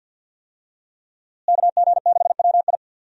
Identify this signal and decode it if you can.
Morse code, heard as a keyed tone, about 49 wpm, 700 Hz. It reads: KG6WI
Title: USA Call Signs - 5 Characters - Repeated 3x - 50wpm